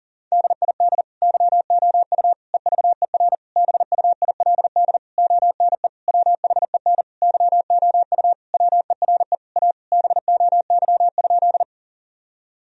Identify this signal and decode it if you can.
Morse code, heard as a keyed tone, about 40 wpm, 700 Hz. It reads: DID YOU EVER BUILD ONE WHEN YOU WERE A BOY?
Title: Sentences from Top 1000 Words - Review Part 1 - 40wpm